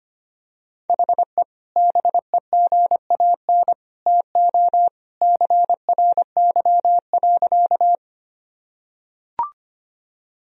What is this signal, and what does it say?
Morse code, heard as a keyed tone, about 25 wpm, 700 Hz. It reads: HE BEGAN TO CRY.
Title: Sentences from Top 1000 Words - Review Part 2 - 25wpm